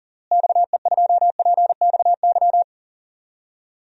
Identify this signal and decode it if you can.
Morse code, heard as a keyed tone, about 37 wpm, 700 Hz. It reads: XE2PXY
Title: DXCC-100 Call Signs - 6 Characters - Repeated 3x - 40wpm